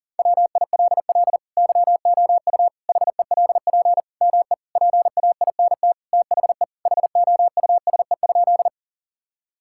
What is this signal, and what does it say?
Morse code, heard as a keyed tone, about 40 wpm, 700 Hz. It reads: WILL YOU HELP ME PAINT THE HOUSE?
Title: Sentences from Top 1000 Words - Review Part 6 - 40wpm